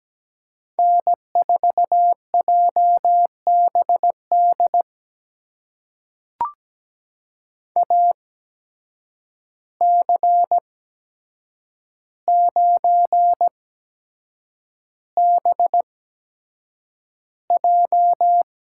Morse code, heard as a keyed tone, about 17 wpm, spaced as for 7 wpm, 700 Hz. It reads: N4JBD AC9BJ
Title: USA Call Signs - 5 Characters - 8x Character Spacing - 17wpm